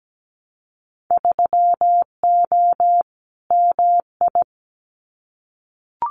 Morse code, heard as a keyed tone, about 17 wpm, 700 Hz. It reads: 3O MI
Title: Winter Field Day Exchanges 17wpm